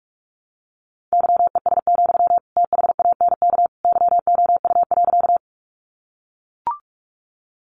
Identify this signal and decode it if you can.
Morse code, heard as a keyed tone, about 44 wpm, 700 Hz. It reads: YES, THANK YOU.
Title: Sentences from Top 600 Words 45wpm